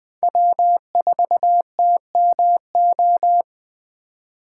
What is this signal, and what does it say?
Morse code, heard as a keyed tone, about 20 wpm, 700 Hz. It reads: W4TMO
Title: K1USN SST QSO Practice - 20wpm